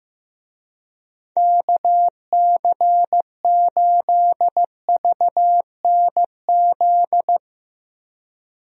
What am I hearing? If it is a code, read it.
Morse code, heard as a keyed tone, about 15 wpm, 700 Hz. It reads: KC8VNZ
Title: USA Call Signs - 6 Characters - 15wpm